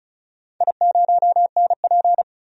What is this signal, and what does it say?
Morse code, heard as a keyed tone, about 36 wpm, 700 Hz. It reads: I0NP
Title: DXCC-100 Call Signs - 4 Characters - 35wpm